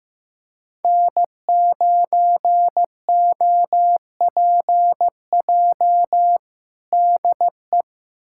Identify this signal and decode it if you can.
Morse code, heard as a keyed tone, about 16 wpm, 700 Hz. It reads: N9OPJ DE
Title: USA Callsign DE Callsign - 5 Characters - 15wpm